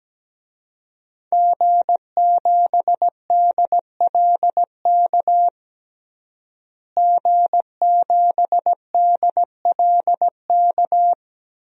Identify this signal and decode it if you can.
Morse code, heard as a keyed tone, about 17 wpm, 700 Hz. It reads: G7DLK G7DLK
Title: DXCC-100 Call Signs - 5 Characters - Repeated 2x - 17wpm